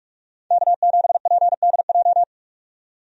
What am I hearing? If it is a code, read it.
Morse code, heard as a keyed tone, about 46 wpm, 700 Hz. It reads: K7PDJ
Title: USA Call Signs - 5 Characters - 45wpm